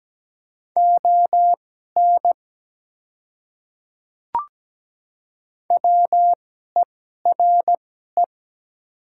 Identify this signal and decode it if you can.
Morse code, heard as a keyed tone, about 17 wpm, spaced as for 12 wpm, 700 Hz. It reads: ON WERE
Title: 2x Character Spacing - Top 100 Words - 17wpm